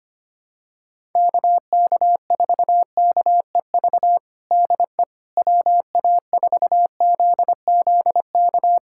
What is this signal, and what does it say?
Morse code, heard as a keyed tone, about 25 wpm, 700 Hz. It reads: KK4KEV DE WA4ZZK
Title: USA Callsign DE Callsign - 6 Characters - 25wpm